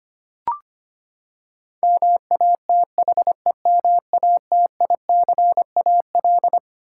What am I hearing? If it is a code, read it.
Morse code, heard as a keyed tone, about 25 wpm, 700 Hz. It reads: MATHEMATICAL
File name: Twelve Letter Words - 25wpm